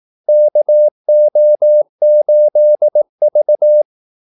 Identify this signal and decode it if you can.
Morse code, heard as a keyed tone, about 18 wpm, 600 Hz. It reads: KO8V
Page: Kaldesignaler 18 wpm